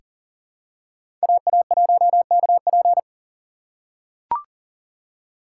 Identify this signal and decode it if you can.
Morse code, heard as a keyed tone, about 40 wpm, 700 Hz. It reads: AA1KP
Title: USA Call Signs - 5 Characters - 6x Character Spacing - 40wpm